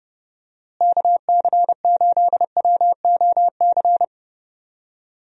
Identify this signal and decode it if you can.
Morse code, heard as a keyed tone, about 32 wpm, 700 Hz. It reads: KC8WOC
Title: USA Call Signs - 6 Characters - Repeated 2x - 30wpm